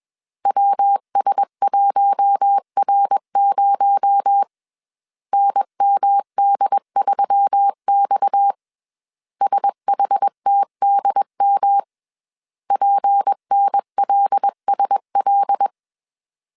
Morse code, heard as a keyed tone, about 21 wpm, 800 Hz. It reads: WS1R0 NMD3X SHTDM PNLSL
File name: CW / DET / 1 6 1500C S A 20wpm 2048